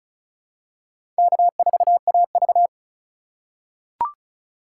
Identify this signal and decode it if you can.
Morse code, heard as a keyed tone, about 36 wpm, 700 Hz. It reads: K4AV
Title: USA Call Signs - 4 Characters - 2x Character Spacing - 35wpm